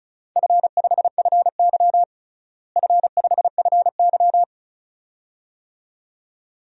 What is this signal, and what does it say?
Morse code, heard as a keyed tone, about 36 wpm, 700 Hz. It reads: F5FY F5FY
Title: DXCC-100 Call Signs - 4 Characters - Repeated 2x - 35wpm